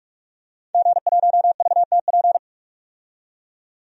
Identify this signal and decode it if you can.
Morse code, heard as a keyed tone, about 43 wpm, 700 Hz. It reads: G1VTP
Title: DXCC-100 Call Signs - 5 Characters - 45wpm